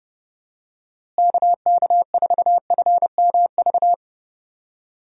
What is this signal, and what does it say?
Morse code, heard as a keyed tone, about 32 wpm, 700 Hz. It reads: KK4FMV
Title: USA Call Signs - 6 Characters - Repeated 2x - 30wpm